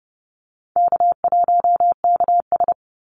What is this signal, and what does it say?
Morse code, heard as a keyed tone, about 30 wpm, 700 Hz. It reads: K1KS
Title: Easy Call Signs - 30wpm